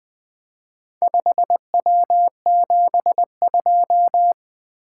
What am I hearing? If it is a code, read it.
Morse code, heard as a keyed tone, about 20 wpm, 700 Hz. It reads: 5W72
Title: Four Letter-Number Combinations - 20wpm